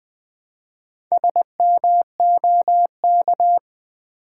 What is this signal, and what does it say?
Morse code, heard as a keyed tone, about 20 wpm, 700 Hz. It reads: SMOK
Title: Build A Word - 1 Letter At A Time - 5-Letter Words - 20wpm